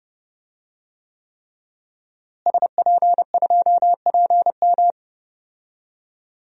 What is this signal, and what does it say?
Morse code, heard as a keyed tone, about 30 wpm, 700 Hz. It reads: SP2PM
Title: DXCC-100 Call Signs - 5 Characters - Repeated 3x - 30wpm